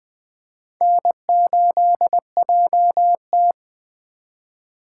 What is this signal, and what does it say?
Morse code, heard as a keyed tone, about 20 wpm, 700 Hz. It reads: N8JT
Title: USA Call Signs - 4 Characters - 2x Character Spacing - 20wpm